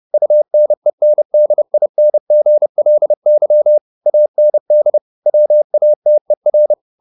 Morse code, heard as a keyed tone, about 30 wpm, 600 Hz. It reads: UNENDINGLY AND WATER
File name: the old man and the sea / the old man and the sea 1 30wpm